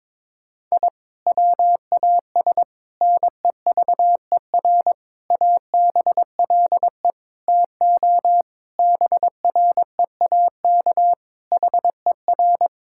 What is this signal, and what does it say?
Morse code, heard as a keyed tone, about 21 wpm, 700 Hz. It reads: I WAS NEVER ABLE TO BREAK HER
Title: Sentences from Top 1000 Words - Review Part 6 - 22wpm